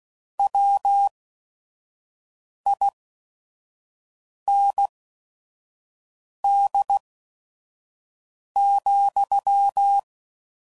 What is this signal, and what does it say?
Morse code, heard as a keyed tone, about 16 wpm, spaced as for 5 wpm, 800 Hz. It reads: WIND,
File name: CW Code Groups @5WPM / Weather Sets / Weather 4 17